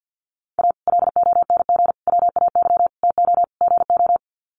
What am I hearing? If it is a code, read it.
Morse code, heard as a keyed tone, about 50 wpm, 700 Hz. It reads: A LONG WAY TO GO
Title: Sets of 5 Words 50wpm